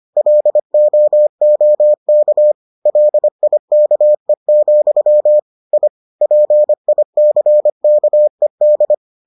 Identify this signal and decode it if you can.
Morse code, heard as a keyed tone, about 25 wpm, 600 Hz. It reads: LOOK LIKE, I PICKED